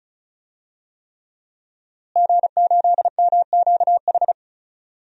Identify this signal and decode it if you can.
Morse code, heard as a keyed tone, about 36 wpm, 700 Hz. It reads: G8MQH
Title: DXCC-100 Call Signs - 5 Characters - Repeated 3x - 35wpm